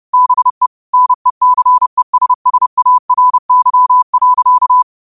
Morse code, heard as a keyed tone, about 30 wpm, 1000 Hz. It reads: BE NECESSARY.